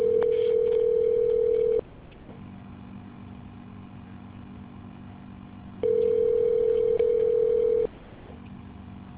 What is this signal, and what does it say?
Ringback tone (North America), 440 + 480 Hz.